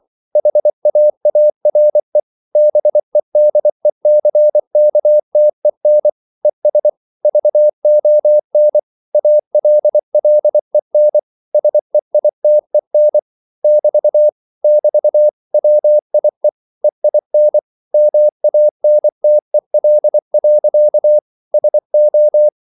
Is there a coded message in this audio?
Yes: Morse code from 0.4 to 22.5 s, heard as a keyed tone, about 24 wpm, 600 Hz. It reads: HAARE BEDECKTEN ES VON ALLEN SEITEN <BT> <BT> WIE EIN MANTEL. SO